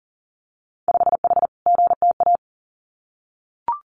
Morse code, heard as a keyed tone, about 40 wpm, 700 Hz. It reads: 5H GTA
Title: Winter Field Day Exchanges Encore 40wpm